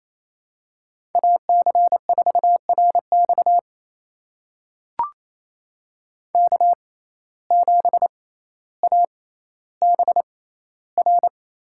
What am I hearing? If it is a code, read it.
Morse code, heard as a keyed tone, about 28 wpm, spaced as for 14 wpm, 700 Hz. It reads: AC4RX K7ABR
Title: USA Call Signs - 5 Characters - 6x Character Spacing - 28wpm